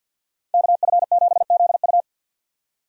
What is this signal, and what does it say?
Morse code, heard as a keyed tone, about 47 wpm, 700 Hz. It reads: KF7ZU
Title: USA Call Signs - 5 Characters - 50wpm